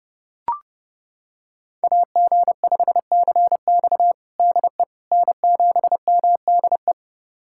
Morse code, heard as a keyed tone, about 32 wpm, 700 Hz. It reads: AG5CX DE N7MDE
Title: USA Callsign DE Callsign - 5 Characters - 30wpm